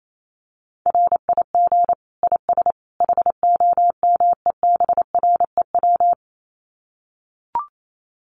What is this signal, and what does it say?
Morse code, heard as a keyed tone, about 28 wpm, 700 Hz. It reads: RIG IS HOMEBREW
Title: Common QSO Phrases 28wpm